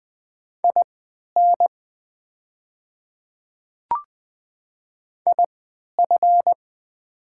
Morse code, heard as a keyed tone, about 20 wpm, spaced as for 11 wpm, 700 Hz. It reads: IN IF
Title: 3x Character Spacing - Top 200 Words - 20wpm